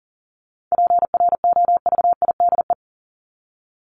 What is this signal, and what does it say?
Morse code, heard as a keyed tone, about 40 wpm, 700 Hz. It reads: PROVIDE
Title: ING Suffix - Top 100 Words with Base Words 40wpm